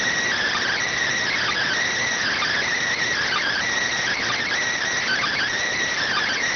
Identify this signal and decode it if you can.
Emergency Alert System (SAME) tones (header bursts), heard as harsh buzzing bursts. It is heard from 1.2 to 5.5 s.